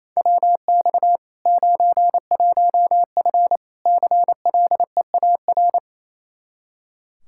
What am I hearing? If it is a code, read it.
Morse code, heard as a keyed tone, about 27 wpm, 700 Hz. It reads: WX 91F CLEAR
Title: Practice QSO - Exchange 2 - Info - 28wpm